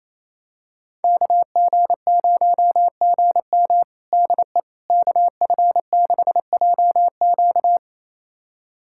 Morse code, heard as a keyed tone, about 28 wpm, 700 Hz. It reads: KG0GM DE KF6JQ
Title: USA Callsign DE Callsign - 5 Characters - 28wpm